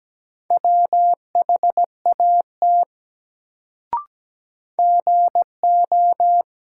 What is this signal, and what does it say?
Morse code, heard as a keyed tone, about 17 wpm, 700 Hz. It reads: WHAT GO
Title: Sentences from Top 100 Words - New Word Drill - 17wpm